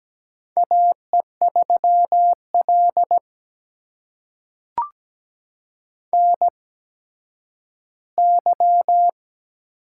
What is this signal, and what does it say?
Morse code, heard as a keyed tone, about 17 wpm, spaced as for 10 wpm, 700 Hz. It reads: AE3L NY
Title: USA Call Signs - 4 Characters - 8x Character Spacing - 17wpm